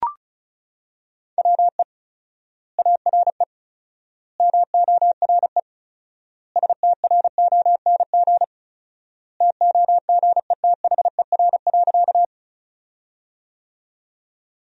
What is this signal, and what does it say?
Morse code, heard as a keyed tone, about 36 wpm, 700 Hz. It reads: WE ARE MORE STRONG TOGETHER.
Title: 4x Word-Spacing - Sentences from Top 500 Words - 35wpm